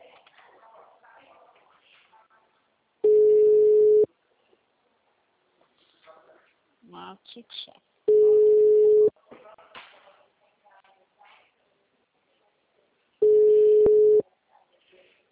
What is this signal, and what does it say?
Ringback tone (Europe, 425 Hz), with speech.
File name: out- -2044- - - .35810